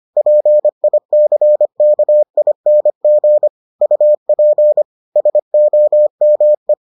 Morse code, heard as a keyed tone, about 25 wpm, 600 Hz. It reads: PICKING UP SOME